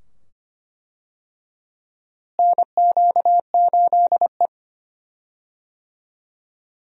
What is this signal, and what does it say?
Morse code, heard as a keyed tone, about 25 wpm, 700 Hz. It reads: NQ8E
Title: USA Call Signs - 4 Characters - Repeated 2x - 25wpm